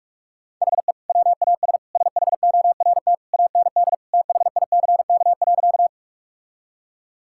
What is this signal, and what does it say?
Morse code, heard as a keyed tone, about 41 wpm, 700 Hz. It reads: HE WAS SHORT AND THICK.
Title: Sentences from Top 1600 Words 45wpm